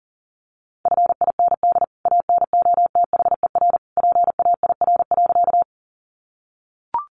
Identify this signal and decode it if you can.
Morse code, heard as a keyed tone, about 40 wpm, 700 Hz. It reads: FIND ANOTHER PAIR.
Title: Sentences from Top 1200 Words 40wpm